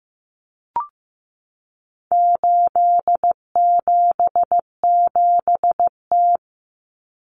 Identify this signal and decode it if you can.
Morse code, heard as a keyed tone, about 15 wpm, 700 Hz. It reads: 877T
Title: Common Cut Numbers - 4-Digit 15wpm